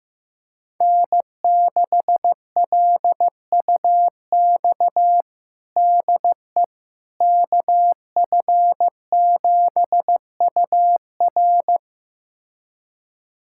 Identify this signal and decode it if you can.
Morse code, heard as a keyed tone, about 15 wpm, 700 Hz. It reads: N6LUX DE KF7UR
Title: USA Callsign DE Callsign - 5 Characters - 15wpm